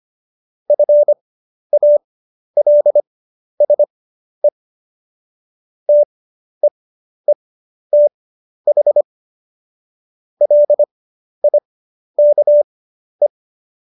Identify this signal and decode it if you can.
Morse code, heard as a keyed tone, about 25 wpm, spaced as for 11 wpm, 590 Hz. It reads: FALSE TEETH LIKE
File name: Jokes-25-char-10-eff